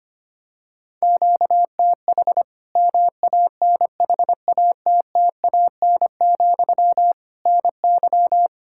Morse code, heard as a keyed tone, about 25 wpm, 700 Hz. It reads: QTH MANHATTAN, NY
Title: Practice QSO - Exchange 1 - Beginning - 25wpm